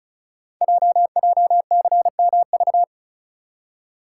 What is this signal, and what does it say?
Morse code, heard as a keyed tone, about 36 wpm, 690 Hz. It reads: JJCMV
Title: Code Groups - 5 Letters - 35wpm